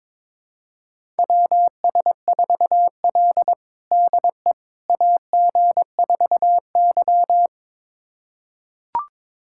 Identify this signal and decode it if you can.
Morse code, heard as a keyed tone, about 22 wpm, 700 Hz. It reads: WS4L DE AG4Y
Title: USA Callsign DE Callsign - 4 Characters - 22wpm